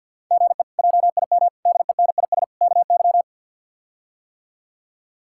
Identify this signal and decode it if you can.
Morse code, heard as a keyed tone, about 51 wpm, 700 Hz. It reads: GE JIM DENIS KY
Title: K1USN SST QSO Practice - 50wpm